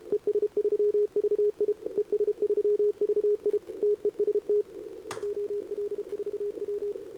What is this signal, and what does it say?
Morse code, heard as a keyed tone, about 29 wpm, 410 Hz. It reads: ES3VI ES3VI TEST OL4W
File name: CQWWCW2025 / CQWWCW2025 80mRozjezd